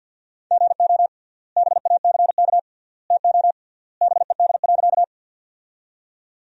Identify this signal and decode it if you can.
Morse code, heard as a keyed tone, about 52 wpm, 700 Hz. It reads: GO BACK TO BED.
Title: 3x Word-Spacing - Sentences from Top 500 Words - 50wpm